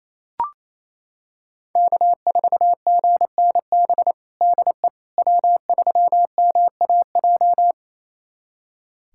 Morse code, heard as a keyed tone, about 28 wpm, 700 Hz. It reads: K4GNB DE W3MAJ
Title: USA Callsign DE Callsign - 5 Characters - 28wpm